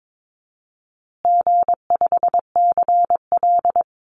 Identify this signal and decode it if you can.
Morse code, heard as a keyed tone, about 22 wpm, 700 Hz. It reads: G5CL
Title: Easy Call Signs Repeated 3x - 22wpm